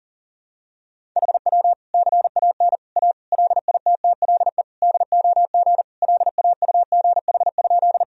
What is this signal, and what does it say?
Morse code, heard as a keyed tone, about 40 wpm, 700 Hz. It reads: HW CAN A LITTLE DOG LAUGH?
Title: Sentences from Top 500 Words and Abbreviations 40wpm